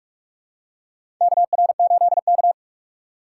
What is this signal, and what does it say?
Morse code, heard as a keyed tone, about 45 wpm, 700 Hz. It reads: KR8K
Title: USA Call Signs - 6x Character Spacing - 45wpm